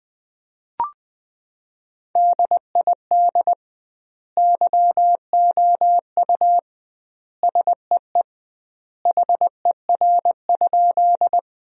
Morse code, heard as a keyed tone, about 20 wpm, 700 Hz. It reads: DID YOU SEE HER?
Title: 2x Word-Spacing - Sentences from Top 100 Words - 20wpm